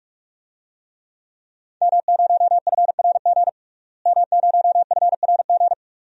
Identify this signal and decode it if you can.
Morse code, heard as a keyed tone, about 46 wpm, 700 Hz. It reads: M0FRG M0FRG
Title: DXCC-100 Call Signs - 5 Characters - Repeated 2x - 45wpm